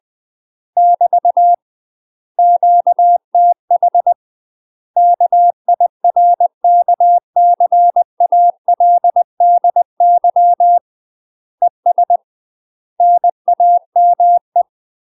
Morse code, heard as a keyed tone, about 20 wpm, 700 Hz. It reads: <BT> QTH KIRKCALDY ES NAME
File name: UK MASTER QSOs TRAINING / 20WPM / UK MASTER QSOs 0049